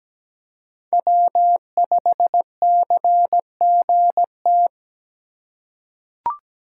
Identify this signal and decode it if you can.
Morse code, heard as a keyed tone, about 17 wpm, 700 Hz. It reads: W5CGT
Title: USA Call Signs - 5 Characters - 2x Character Spacing - 17wpm